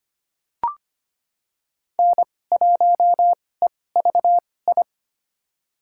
Morse code, heard as a keyed tone, about 25 wpm, spaced as for 18 wpm, 700 Hz. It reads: N1EVI
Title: USA Call Signs - 5 Characters - 2x Character Spacing - 25wpm